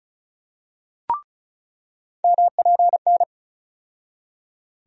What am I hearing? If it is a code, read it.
Morse code, heard as a keyed tone, about 36 wpm, 700 Hz. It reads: MPN
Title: USA Call Sign Postfixes - 3 Letters 35wpm